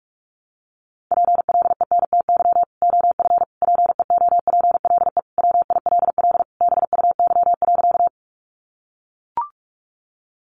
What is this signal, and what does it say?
Morse code, heard as a keyed tone, about 44 wpm, 700 Hz. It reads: PLENTY OF PEOPLE WILL BUY.
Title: Sentences from Top 1300 Words 45wpm